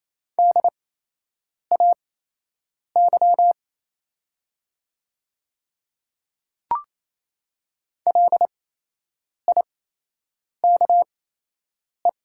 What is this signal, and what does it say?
Morse code, heard as a keyed tone, about 27 wpm, spaced as for 8 wpm, 700 Hz. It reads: DAY LIKE
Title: 8x Character Spacing - Top 200 Words - 28wpm